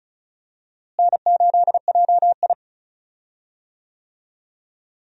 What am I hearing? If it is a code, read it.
Morse code, heard as a keyed tone, about 36 wpm, 700 Hz. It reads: N8JI
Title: USA Call Signs - 4 Characters - Repeated 3x - 35wpm